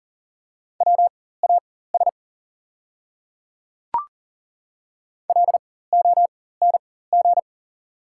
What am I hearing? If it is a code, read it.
Morse code, heard as a keyed tone, about 39 wpm, spaced as for 19 wpm, 700 Hz. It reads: WAS LONG
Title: 4x Character Spacing - Top 200 Words - 40wpm